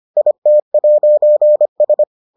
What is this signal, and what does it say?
Morse code, heard as a keyed tone, about 25 wpm, 600 Hz. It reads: IT'S